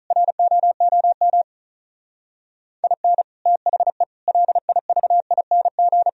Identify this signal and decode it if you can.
Morse code, heard as a keyed tone, about 44 wpm, 700 Hz. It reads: ROOM IN THE LIVING